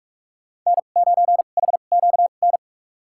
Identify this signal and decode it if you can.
Morse code, heard as a keyed tone, about 43 wpm, spaced as for 32 wpm, 700 Hz. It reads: N9HQN
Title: USA Call Signs - 2x Character Spacing - 45wpm